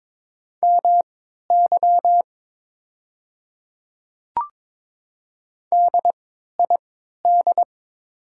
Morse code, heard as a keyed tone, about 22 wpm, spaced as for 12 wpm, 690 Hz. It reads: MY DID
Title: 3x Character Spacing - Top 100 Words - 22wpm